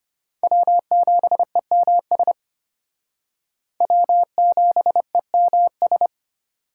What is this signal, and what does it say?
Morse code, heard as a keyed tone, about 31 wpm, 700 Hz. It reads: W7EMS W7EMS
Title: Speed Racing - USA 1x3 Call Signs 25wpm